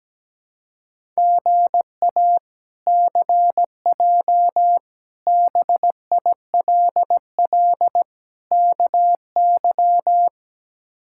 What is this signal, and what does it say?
Morse code, heard as a keyed tone, about 17 wpm, 700 Hz. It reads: GA CJ BILL KY
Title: K1USN SST QSO Practice - 17wpm